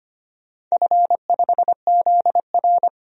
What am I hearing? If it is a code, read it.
Morse code, heard as a keyed tone, about 25 wpm, 700 Hz. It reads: F5ZR
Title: DXCC-100 Call Signs - 4 Characters - 25wpm